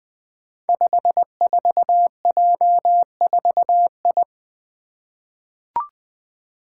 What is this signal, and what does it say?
Morse code, heard as a keyed tone, about 20 wpm, 700 Hz. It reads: 54J4I
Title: Five Challenging Letter-Number Combinations 20wpm